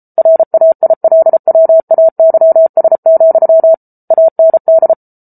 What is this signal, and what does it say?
Morse code, heard as a keyed tone, about 34 wpm, 650 Hz. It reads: RAILWAYS, AND